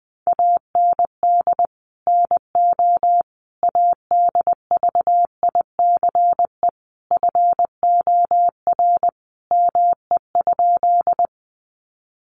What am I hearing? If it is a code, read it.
Morse code, heard as a keyed tone, about 20 wpm, 700 Hz. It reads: AND NO ADVICE FOR ME?
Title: Sentences from Top 1100 Words 20wpm